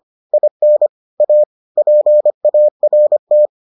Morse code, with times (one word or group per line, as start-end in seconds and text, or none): 0.3-0.9 IN
1.2-1.4 A
1.8-3.5 PART